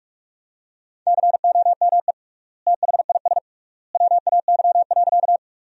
Morse code, heard as a keyed tone, about 46 wpm, 700 Hz. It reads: COME THIS WAY.
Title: 3x Word-Spacing - Sentences from Top 100 Words - 45wpm